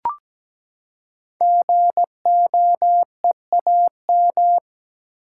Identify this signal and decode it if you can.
Morse code, heard as a keyed tone, about 17 wpm, 700 Hz. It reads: GOEAM
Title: Code Groups - 5 Letters - 17wpm